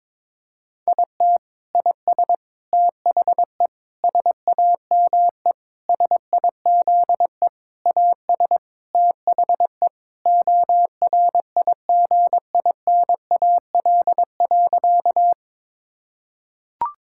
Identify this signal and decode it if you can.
Morse code, heard as a keyed tone, about 22 wpm, 700 Hz. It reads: IT IS THE SAME SIZE AS THE ORIGINAL.
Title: Sentences from Top 1000 Words - Review Part 6 - 22wpm